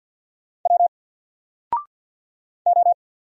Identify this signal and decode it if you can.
Morse code, heard as a keyed tone, about 48 wpm, 700 Hz. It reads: W O
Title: Single Character (Letter:Number:Punctuation:Prosign) 50wpm